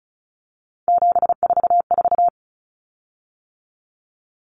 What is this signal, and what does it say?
Morse code, heard as a keyed tone, about 34 wpm, 700 Hz. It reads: 744
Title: Common Cut Numbers - 3-Digit 35wpm